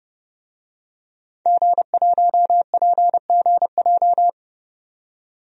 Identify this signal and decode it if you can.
Morse code, heard as a keyed tone, about 32 wpm, 700 Hz. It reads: G1PGJ
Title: DXCC-100 Call Signs - 5 Characters - 30wpm